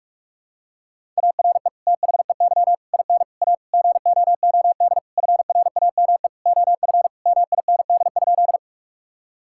Morse code, heard as a keyed tone, about 46 wpm, 700 Hz. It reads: ARE THEY IN A GOOD FRAME OF MIND?
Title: Sentences from Top 1000 Words - Review Part 6 - 45wpm